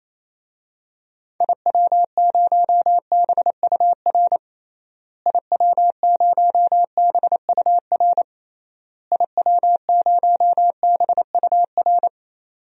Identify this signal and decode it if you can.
Morse code, heard as a keyed tone, about 28 wpm, 700 Hz. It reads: IW0BUR IW0BUR IW0BUR
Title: DXCC-100 Call Signs - 6 Characters - Repeated 3x - 28wpm